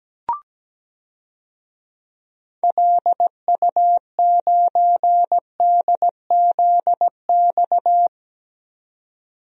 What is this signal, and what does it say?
Morse code, heard as a keyed tone, about 17 wpm, 700 Hz. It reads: LU9DZX
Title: DXCC-100 Call Signs - 6 Characters - 17wpm